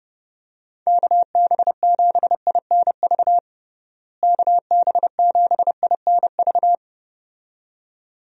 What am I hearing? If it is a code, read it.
Morse code, heard as a keyed tone, about 31 wpm, 700 Hz. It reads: KB7INV KB7INV
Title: USA Call Signs - 6 Characters - Repeated 2x - 30wpm